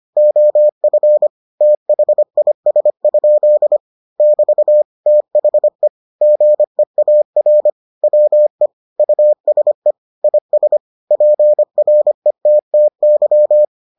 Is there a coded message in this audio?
Yes: Morse code from 0.2 to 13.6 s, heard as a keyed tone, about 25 wpm, 600 Hz. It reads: OF THIS? <BT> THE GEAR WE USE IS PRETTY